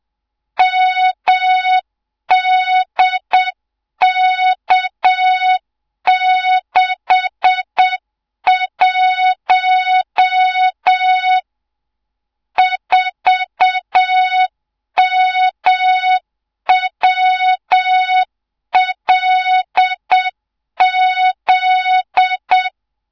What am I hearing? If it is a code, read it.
Morse code, heard as a keyed tone, about 7 wpm, 3700 Hz, with noise. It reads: MDK61 4MWLZ
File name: Morze-HA2MN / hang / 24 GYAKORLAT